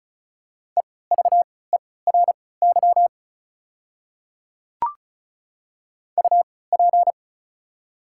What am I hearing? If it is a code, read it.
Morse code, heard as a keyed tone, about 36 wpm, spaced as for 20 wpm, 700 Hz. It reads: EVERY UP
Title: 3x Character Spacing - Top 200 Words - 35wpm